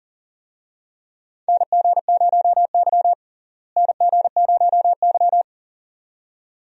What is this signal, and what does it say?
Morse code, heard as a keyed tone, about 40 wpm, 700 Hz. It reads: NG0Y NG0Y
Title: USA Call Signs - 4 Characters - Repeated 2x - 40wpm